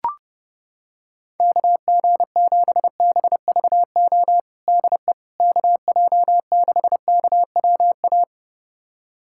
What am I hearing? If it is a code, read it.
Morse code, heard as a keyed tone, about 30 wpm, 700 Hz. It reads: KG7BVO DE KJ6KWA
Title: USA Callsign DE Callsign - 6 Characters - 30wpm